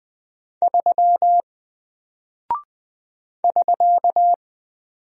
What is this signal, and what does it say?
Morse code, heard as a keyed tone, about 21 wpm, 700 Hz. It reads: 3 <SK>
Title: Single Character (Letter:Number:Punctuation:Prosign) Alt 20wpm